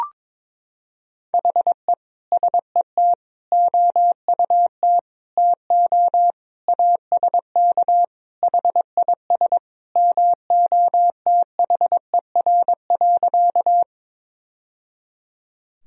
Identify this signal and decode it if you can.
Morse code, heard as a keyed tone, about 22 wpm, 700 Hz. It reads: HE SET OUT TO ASK HIS MOTHER.
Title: Sentences from Top 1000 Words - Review Part 1 - 22wpm